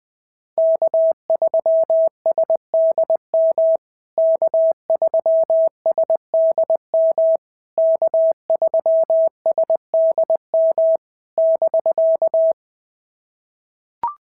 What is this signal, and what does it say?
Morse code, heard as a keyed tone, about 20 wpm, 650 Hz. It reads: K3SDM K3SDM K3SDM <BK>
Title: POTA QSO Practice - 20wpm